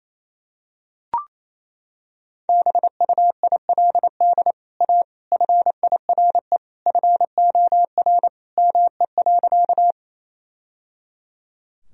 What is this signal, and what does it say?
Morse code, heard as a keyed tone, about 28 wpm, 700 Hz. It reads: BUILD A FIRE FOR ME.
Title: Sentences from Top 1000 Words - Review Part 1 - 28wpm